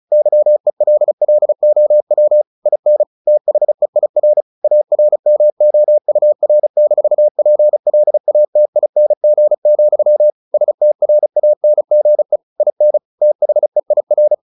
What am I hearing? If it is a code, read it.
Morse code, heard as a keyed tone, about 34 wpm, 600 Hz. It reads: YELLOW IN THEIR ARMOUR-PLATING, STRANGE IN THEIR
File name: the old man and the sea / the old man and the sea 1 35wpm